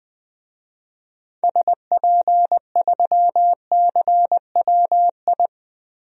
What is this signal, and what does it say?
Morse code, heard as a keyed tone, about 20 wpm, 700 Hz. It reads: SP3CWI
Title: DXCC-100 Call Signs - 6 Characters - 20wpm